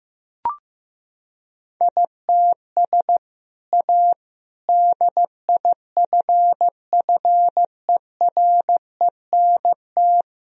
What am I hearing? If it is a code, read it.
Morse code, heard as a keyed tone, about 16 wpm, 700 Hz. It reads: ITS A DIFFERENT
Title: Sentences from Top 1000 Words - Review Part 3 - 15wpm